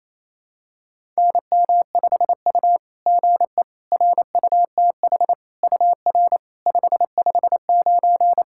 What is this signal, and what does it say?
Morse code, heard as a keyed tone, about 28 wpm, 690 Hz. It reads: NM5U GE RUTH UR 559
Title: SOTA QSO Practice 28wpm